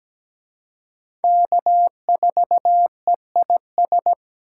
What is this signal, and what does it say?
Morse code, heard as a keyed tone, about 20 wpm, 700 Hz. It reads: K4EIS
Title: Speed Racing - USA 1x3 Call Signs 17wpm